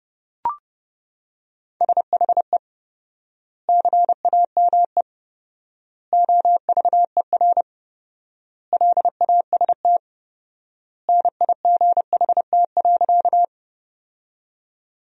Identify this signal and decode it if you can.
Morse code, heard as a keyed tone, about 30 wpm, 700 Hz. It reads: SHE CAME OVER LAST NIGHT.
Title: 4x Word-Spacing - Sentences from Top 300 Words - 30wpm